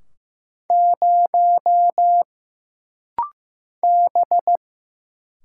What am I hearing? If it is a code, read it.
Morse code, heard as a keyed tone, about 15 wpm, 700 Hz. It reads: 0 B
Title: Single Letter-Number Alt - Rapid-Fire 15wpm